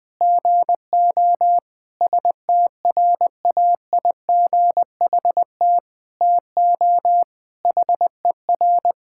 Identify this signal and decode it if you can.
Morse code, heard as a keyed tone, about 20 wpm, 700 Hz. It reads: GO STRAIGHT TO HER
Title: Sentences from Top 1000 Words - Review Part 6 - 20wpm